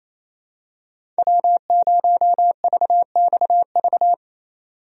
Morse code, heard as a keyed tone, about 28 wpm, 700 Hz. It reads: W0VXV
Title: USA Call Signs - 5 Characters - 28wpm